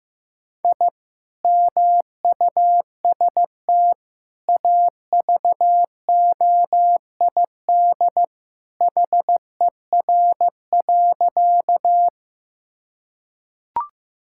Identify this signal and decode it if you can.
Morse code, heard as a keyed tone, about 15 wpm, 700 Hz. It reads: I MUST AVOID HER.
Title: Sentences from Top 1000 Words - Review Part 6 - 15wpm